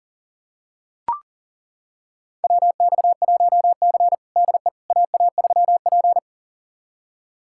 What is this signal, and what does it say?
Morse code, heard as a keyed tone, about 37 wpm, 700 Hz. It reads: WX1C DE AA3P
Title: USA Callsign DE Callsign - 4 Characters - 40wpm